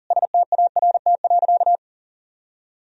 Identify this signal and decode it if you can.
Morse code, heard as a keyed tone, about 40 wpm, 700 Hz. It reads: START.
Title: Sentences from Top 300 Words and Abbreviations 40wpm